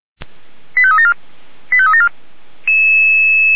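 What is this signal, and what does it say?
Five-tone selective call (ZVEI), heard as quick runs of five tones, tones of about 70 ms. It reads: 85374 85374
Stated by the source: ZVEI five-tone selective call received on an SDR.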